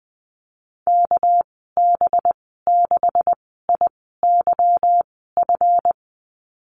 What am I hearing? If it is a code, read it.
Morse code, heard as a keyed tone, about 20 wpm, spaced as for 15 wpm, 700 Hz. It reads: KB6IYF
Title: 207 Lesson 33g - Letter j - Call Signs 20wpm